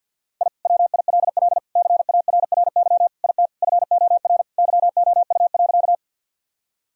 Morse code, heard as a keyed tone, about 50 wpm, 700 Hz. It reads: I WILL CARRY IT FOR YOU.
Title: Sentences from Top 1000 Words - Review Part 1 - 50wpm